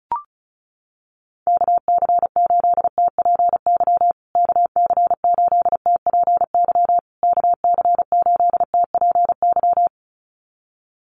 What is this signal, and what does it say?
Morse code, heard as a keyed tone, about 34 wpm, 700 Hz. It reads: KC8TPY KC8TPY KC8TPY
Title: USA Call Signs Repeated 3x 35wpm